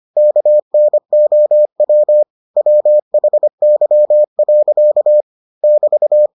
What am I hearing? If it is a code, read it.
Morse code, heard as a keyed tone, about 25 wpm, 600 Hz. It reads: KNOW WHY. <BT>